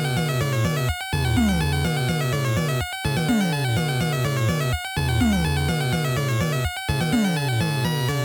There is no code.